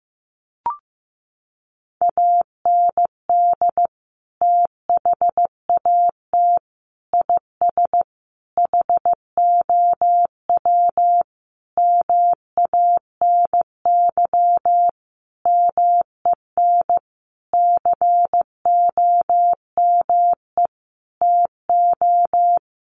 Morse code, heard as a keyed tone, about 15 wpm, 700 Hz. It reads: AND THAT IS HOW MANY MEN COME TO
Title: Sentences from Top 1400 Words 15wpm